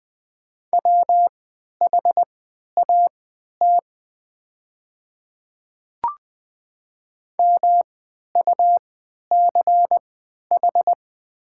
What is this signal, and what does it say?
Morse code, heard as a keyed tone, about 20 wpm, spaced as for 11 wpm, 700 Hz. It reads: WHAT MUCH
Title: 3x Character Spacing - Top 200 Words - 20wpm